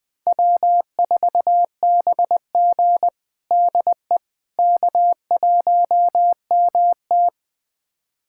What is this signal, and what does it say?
Morse code, heard as a keyed tone, about 21 wpm, 700 Hz. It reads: W4BG DE K1MT
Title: USA Callsign DE Callsign - 4 Characters - 20wpm